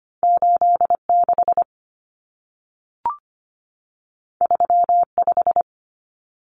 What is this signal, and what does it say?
Morse code, heard as a keyed tone, about 25 wpm, 700 Hz. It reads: 86 35
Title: Common Cut Numbers - 2-Digit 25wpm